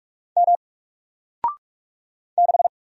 Morse code, heard as a keyed tone, about 45 wpm, 700 Hz. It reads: M 6
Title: Single Character (Letter:Number:Punctuation:Prosign) Alt 45wpm